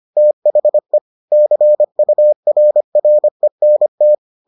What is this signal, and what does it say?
Morse code, heard as a keyed tone, about 25 wpm, 600 Hz. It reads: THE CURRENT